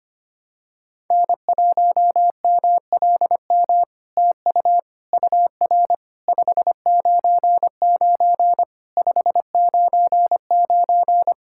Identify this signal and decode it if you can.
Morse code, heard as a keyed tone, about 25 wpm, 700 Hz. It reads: N1MLM TU UR 599 599
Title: SOTA QSO Practice 25wpm